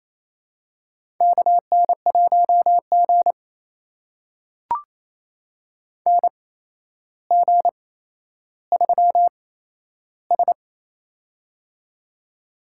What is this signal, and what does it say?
Morse code, heard as a keyed tone, about 28 wpm, spaced as for 12 wpm, 700 Hz. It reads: KN1G NG3S
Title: USA Call Signs - 4 Characters - 8x Character Spacing - 28wpm